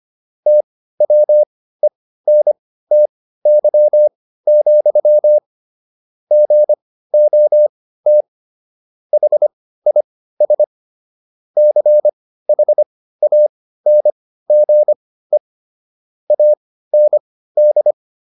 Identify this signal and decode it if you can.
Morse code, heard as a keyed tone, about 25 wpm, spaced as for 15 wpm, 600 Hz. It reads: TWENTY, GOT HIS CHANGE AND